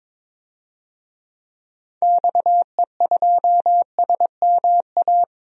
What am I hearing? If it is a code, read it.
Morse code, heard as a keyed tone, about 22 wpm, 700 Hz. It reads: XE2SMA
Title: DXCC-100 Call Signs - 6 Characters - 22wpm